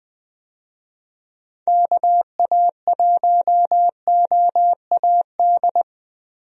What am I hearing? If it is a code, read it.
Morse code, heard as a keyed tone, about 20 wpm, 700 Hz. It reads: KA1OAD
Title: USA Call Signs - 6 Characters - Repeated 3x - 20wpm